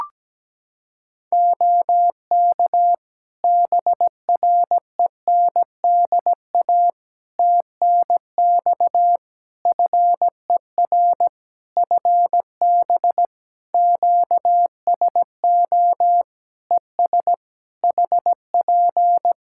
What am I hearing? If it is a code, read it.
Morse code, heard as a keyed tone, about 17 wpm, 700 Hz. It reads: OK BRENDA TNX FER FB QSO ES HP
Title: Practice QSO - Exchange 4 - Ending - 17wpm